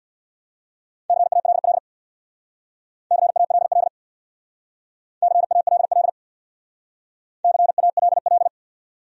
Morse code, heard as a keyed tone, about 70 wpm, 700 Hz. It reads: CALL CALL CALL CALL